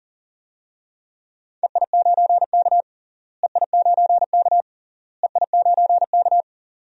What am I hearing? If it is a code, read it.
Morse code, heard as a keyed tone, about 40 wpm, 700 Hz. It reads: EI9K EI9K EI9K
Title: DXCC-100 Call Signs - 4 Characters - Repeated 3x - 40wpm